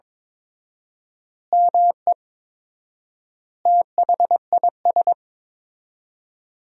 Morse code, heard as a keyed tone, about 22 wpm, 700 Hz. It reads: ME THIS
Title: 4x Word-Spacing - Sentences from Top 400 Words - 22wpm